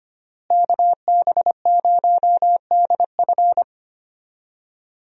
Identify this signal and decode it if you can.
Morse code, heard as a keyed tone, about 25 wpm, 700 Hz. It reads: KB0DF
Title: USA Call Signs - 5 Characters - Repeated 2x - 25wpm